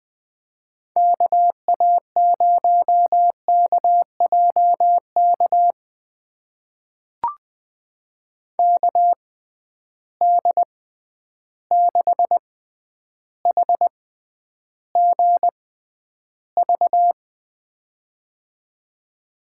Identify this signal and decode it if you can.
Morse code, heard as a keyed tone, about 21 wpm, spaced as for 10 wpm, 700 Hz. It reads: KA0KJK KD6HGV
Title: USA Call Signs - 6 Characters - 6x Character Spacing - 20wpm